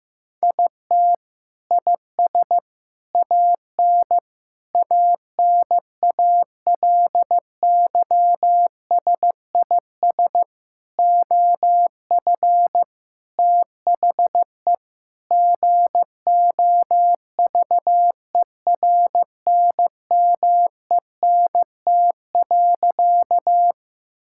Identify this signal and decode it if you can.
Morse code, heard as a keyed tone, about 15 wpm, 700 Hz. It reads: IT IS AN ANALYSIS OF THE GOVERNMENT.
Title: Sentences from Top 1000 Words - Review Part 5 - 15wpm